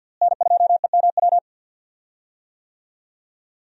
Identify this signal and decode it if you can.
Morse code, heard as a keyed tone, about 51 wpm, 700 Hz. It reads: N2EMW
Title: USA Call Signs - 5 Characters - Repeated 2x - 50wpm